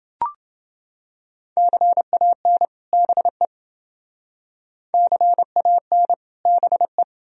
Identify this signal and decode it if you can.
Morse code, heard as a keyed tone, about 30 wpm, 700 Hz. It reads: CAN BE CAN BE